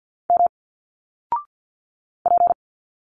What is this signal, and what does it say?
Morse code, heard as a keyed tone, about 50 wpm, 700 Hz. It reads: M P
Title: Single Letters 50wpm